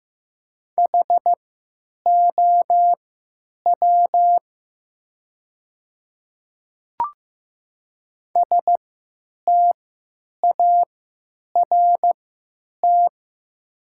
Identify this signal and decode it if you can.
Morse code, heard as a keyed tone, about 15 wpm, spaced as for 9 wpm, 700 Hz. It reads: HOW START
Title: 3x Character Spacing - Top 300 Words - 15wpm